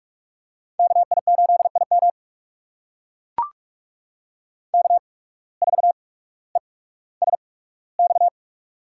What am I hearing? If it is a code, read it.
Morse code, heard as a keyed tone, about 44 wpm, spaced as for 19 wpm, 700 Hz. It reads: KI8IM K4ESX
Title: USA Call Signs - 5 Characters - 8x Character Spacing - 45wpm